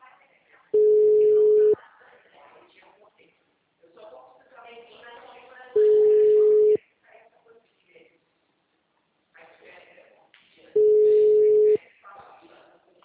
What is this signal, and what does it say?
Ringback tone (Europe, 425 Hz), with speech.
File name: out- -2006- - - .27990